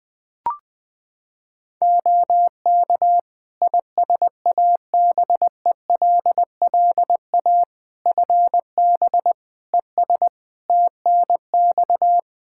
Morse code, heard as a keyed tone, about 21 wpm, 700 Hz. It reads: OK ISABELLA FB ES TNX
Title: Practice QSO - Exchange 2 - Info - 20wpm